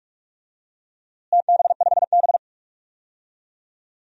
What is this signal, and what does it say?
Morse code, heard as a keyed tone, about 45 wpm, 700 Hz. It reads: TB5B
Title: DXCC-100 Call Signs - 4 Characters - Repeated 2x - 45wpm